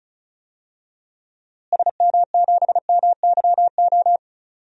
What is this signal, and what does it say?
Morse code, heard as a keyed tone, about 36 wpm, 700 Hz. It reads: SM7MYO
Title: DXCC-100 Call Signs - 6 Characters - Repeated 2x - 35wpm